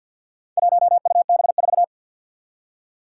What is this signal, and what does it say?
Morse code, heard as a keyed tone, about 47 wpm, 700 Hz. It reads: 1UB4
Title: Four Challenging Letter-Number Combinations 50wpm